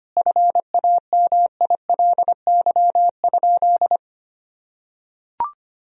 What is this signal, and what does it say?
Morse code, heard as a keyed tone, about 25 wpm, 700 Hz. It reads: FAMILY?
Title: Sentences from Top 1000 Words - Review Part 4 - 25wpm